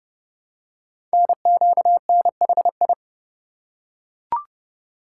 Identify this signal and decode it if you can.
Morse code, heard as a keyed tone, about 30 wpm, 690 Hz. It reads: NQNHI
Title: Code Groups - 5 Letters - 30wpm